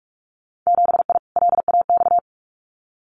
Morse code, heard as a keyed tone, about 46 wpm, 700 Hz. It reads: 7I LAX
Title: Winter Field Day Exchanges Encore 45wpm